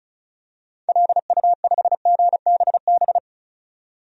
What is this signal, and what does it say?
Morse code, heard as a keyed tone, about 34 wpm, 700 Hz. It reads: LU5GBB
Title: DXCC-100 Call Signs - 6 Characters - 35wpm